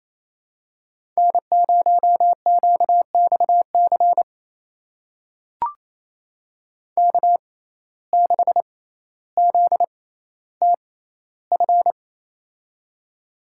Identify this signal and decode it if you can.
Morse code, heard as a keyed tone, about 28 wpm, spaced as for 14 wpm, 700 Hz. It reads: N0QXC K6ZTF
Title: USA Call Signs - 5 Characters - 6x Character Spacing - 28wpm